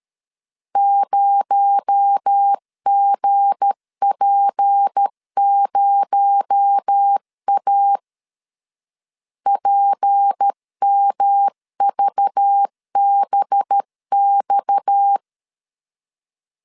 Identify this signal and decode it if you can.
Morse code, heard as a keyed tone, about 13 wpm, 800 Hz. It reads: 0GP0A PMVBX
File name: LEZIONI / 7 GF9Z / 7 G 12wpm 2055